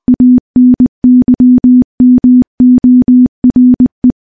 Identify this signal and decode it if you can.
Morse code, heard as a keyed tone, about 21 wpm, 250 Hz. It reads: ANYMORE
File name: morse / morse-training-20WPM